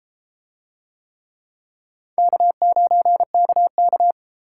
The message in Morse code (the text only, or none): K9KK